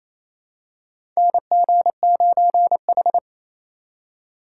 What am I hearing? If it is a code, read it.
Morse code, heard as a keyed tone, about 28 wpm, 700 Hz. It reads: NG9H
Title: USA Call Signs - 4 Characters - 4x Character Spacing - 28wpm